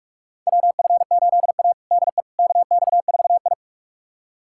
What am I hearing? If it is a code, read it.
Morse code, heard as a keyed tone, about 47 wpm, 700 Hz. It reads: WF8A DE KX4I
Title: USA Callsign DE Callsign - 4 Characters - 45wpm